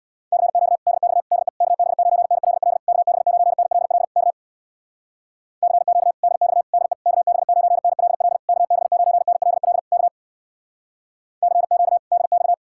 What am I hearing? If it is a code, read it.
Morse code, heard as a keyed tone, about 63 wpm, 700 Hz. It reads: CQ DX DE KB9NLL KB9NLL K CQ DX DE KB9NLL KB9NLL K CQ DX